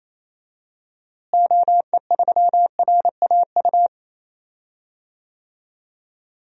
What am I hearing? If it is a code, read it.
Morse code, heard as a keyed tone, about 28 wpm, 700 Hz. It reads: OE3RAU
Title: DXCC-100 Call Signs - 6 Characters - Repeated 3x - 28wpm